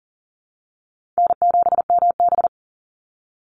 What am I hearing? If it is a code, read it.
Morse code, heard as a keyed tone, about 40 wpm, 700 Hz. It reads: N7MB
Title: Easy Call Signs Repeated 2x - 40wpm